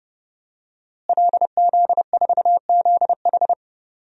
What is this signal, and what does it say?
Morse code, heard as a keyed tone, about 30 wpm, 700 Hz. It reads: LZ4ZH
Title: DXCC-100 Call Signs - 5 Characters - Repeated 3x - 30wpm